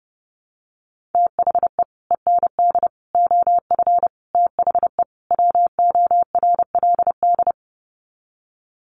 Morse code, heard as a keyed tone, about 29 wpm, 700 Hz. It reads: THE END OF THE WORLD
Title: Sets of 5 Words 30wpm